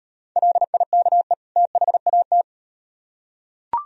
Morse code, heard as a keyed tone, about 38 wpm, 700 Hz. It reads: LIKE THAT